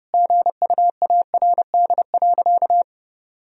Morse code, heard as a keyed tone, about 32 wpm, 700 Hz. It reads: GUARD.
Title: Sentences from Top 1000 Words - Review Part 4 - 30wpm